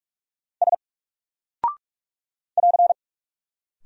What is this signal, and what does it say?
Morse code, heard as a keyed tone, about 48 wpm, 700 Hz. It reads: S <AR>
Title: Single Character (Letter:Number:Punctuation:Prosign) Alt 45wpm